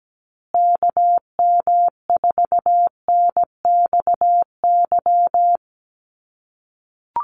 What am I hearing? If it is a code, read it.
Morse code, heard as a keyed tone, about 17 wpm, 700 Hz. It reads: KM4NXY
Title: USA Call Signs Repeated 2x 17wpm